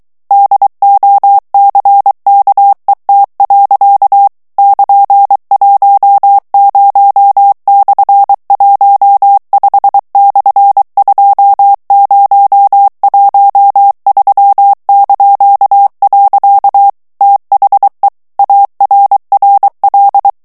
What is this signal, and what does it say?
Morse code, heard as a keyed tone, about 23 wpm, 800 Hz. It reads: DOCKET. <KN>10/15/2013). THE ARRL
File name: ANGLAIS texte clair 3 vitesse 22 mots minute